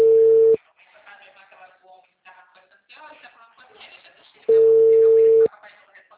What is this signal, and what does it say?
Ringback tone (China, 450 Hz), with speech.